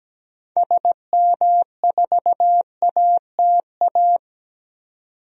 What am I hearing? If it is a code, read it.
Morse code, heard as a keyed tone, about 17 wpm, 700 Hz. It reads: SM4ATA
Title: DXCC-100 Call Signs - 6 Characters - 17wpm